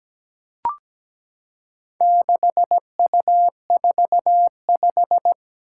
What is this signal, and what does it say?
Morse code, heard as a keyed tone, about 17 wpm, 700 Hz. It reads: 6U45
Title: Four Challenging Letter-Number Combinations 17wpm